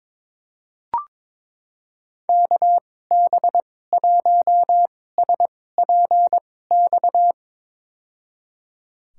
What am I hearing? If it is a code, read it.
Morse code, heard as a keyed tone, about 21 wpm, spaced as for 16 wpm, 700 Hz. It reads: KB1SPX
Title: USA Call Signs - 6 Characters - 2x Character Spacing - 22wpm